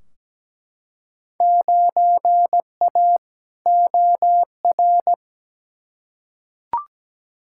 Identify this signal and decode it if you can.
Morse code, heard as a keyed tone, about 17 wpm, 700 Hz. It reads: 9A OR
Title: Field Day Exchanges Encore - 17wpm